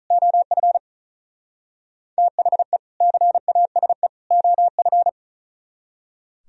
Morse code, heard as a keyed tone, about 35 wpm, 700 Hz. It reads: OF THE CASE OF
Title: Speed-Racing - Sets of 3 Words 35wpm